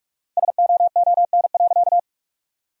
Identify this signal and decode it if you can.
Morse code, heard as a keyed tone, about 47 wpm, 700 Hz. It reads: SOON.
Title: 4x Word-Spacing - Sentences from Top 500 Words - 45wpm